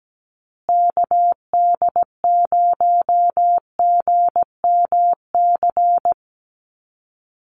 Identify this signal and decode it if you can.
Morse code, heard as a keyed tone, about 17 wpm, 700 Hz. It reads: KD0GMC
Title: USA Call Signs Repeated 3x 17wpm